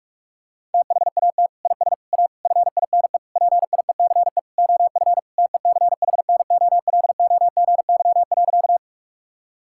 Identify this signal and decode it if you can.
Morse code, heard as a keyed tone, about 46 wpm, 700 Hz. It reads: THAT IS A FINE PIECE OF TECHNOLOGY.
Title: Sentences from Top 1000 Words - Review Part 5 - 45wpm